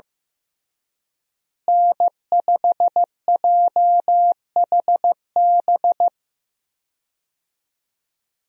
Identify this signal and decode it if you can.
Morse code, heard as a keyed tone, about 15 wpm, 700 Hz. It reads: N5JHB
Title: USA Call Signs - 5 Characters - Repeated 3x - 15wpm